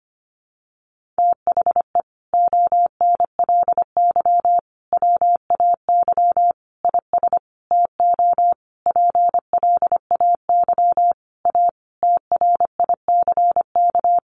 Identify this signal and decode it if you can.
Morse code, heard as a keyed tone, about 25 wpm, 700 Hz. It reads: THE ONLY WAY IS TO PLAY A TRICK
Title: Sentences from Top 1500 Words 25wpm